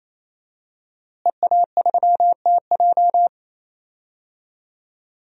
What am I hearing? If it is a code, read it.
Morse code, heard as a keyed tone, about 28 wpm, 700 Hz. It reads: EA3TJ
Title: DXCC-100 Call Signs - 5 Characters - Repeated 3x - 28wpm